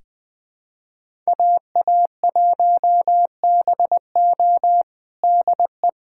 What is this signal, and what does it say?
Morse code, heard as a keyed tone, about 21 wpm, 700 Hz. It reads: AA1BO DE
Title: USA Callsign DE Callsign - 5 Characters - 20wpm